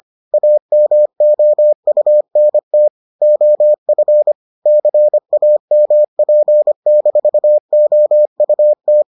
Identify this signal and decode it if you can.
Morse code, heard as a keyed tone, about 25 wpm, 600 Hz. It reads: AMOUNT OF CAMP-OUT